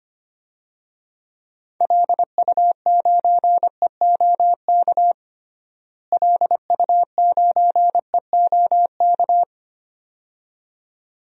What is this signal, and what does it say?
Morse code, heard as a keyed tone, about 25 wpm, 700 Hz. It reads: LU9EOK LU9EOK
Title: DXCC-100 Call Signs - 6 Characters - Repeated 2x - 25wpm